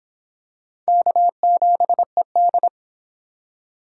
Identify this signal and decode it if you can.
Morse code, heard as a keyed tone, about 25 wpm, 700 Hz. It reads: K7ED